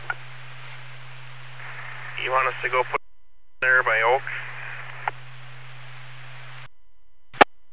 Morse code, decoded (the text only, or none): none